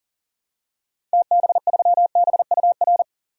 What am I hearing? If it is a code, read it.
Morse code, heard as a keyed tone, about 39 wpm, 700 Hz. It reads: TB3BUR
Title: DXCC-100 Call Signs - 6 Characters - 40wpm